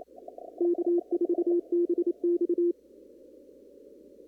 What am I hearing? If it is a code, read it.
Morse code, heard as a keyed tone, about 28 wpm, 350 Hz. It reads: K4BX
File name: CW / kt5j wpx cw-op-k5pi